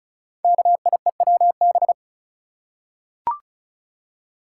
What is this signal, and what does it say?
Morse code, heard as a keyed tone, about 36 wpm, 690 Hz. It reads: KIEWB
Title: Code Groups - 5 Letters - 35wpm